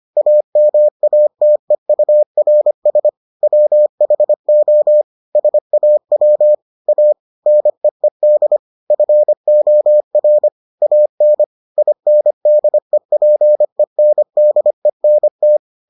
Morse code, heard as a keyed tone, about 25 wpm, 600 Hz. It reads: AMATEURS WHO SAW A NEED FOR AN INDEPENDENT